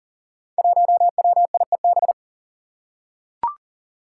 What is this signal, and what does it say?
Morse code, heard as a keyed tone, about 40 wpm, 700 Hz. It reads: 1WIEB
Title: Five Challenging Letter-Number Combinations 40wpm